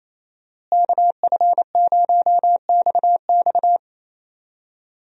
Morse code, heard as a keyed tone, about 28 wpm, 700 Hz. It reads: KF0XX
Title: USA Call Signs - 5 Characters - Repeated 2x - 28wpm